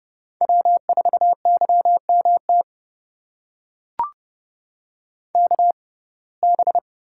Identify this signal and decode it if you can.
Morse code, heard as a keyed tone, about 31 wpm, spaced as for 22 wpm, 700 Hz. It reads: W4YMT KB
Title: USA Call Signs - 5 Characters - 6x Character Spacing - 30wpm